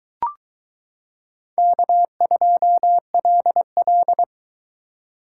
Morse code, heard as a keyed tone, about 22 wpm, 700 Hz. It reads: K2LL